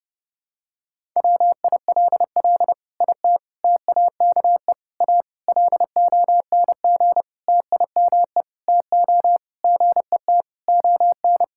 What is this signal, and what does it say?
Morse code, heard as a keyed tone, about 30 wpm, 700 Hz. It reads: WILL IT TAKE A LONG TIME TO GET ON
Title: Sentences from Top 1000 Words - Review Part 6 - 30wpm